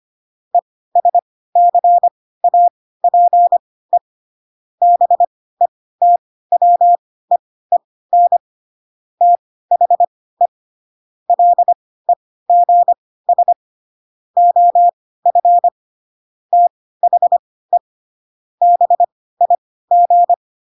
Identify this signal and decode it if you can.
Morse code, heard as a keyed tone, about 25 wpm, spaced as for 16 wpm, 700 Hz. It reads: ESCAPE BETWEEN THE LEGS OF THE BIG
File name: Pinocchio / Pinocchio ch3 10 25 ik1hge